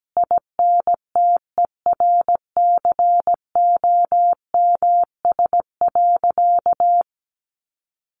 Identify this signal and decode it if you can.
Morse code, heard as a keyed tone, about 17 wpm, 700 Hz. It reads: INTERCOMS.
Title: A Sound Of Thunder 17wpm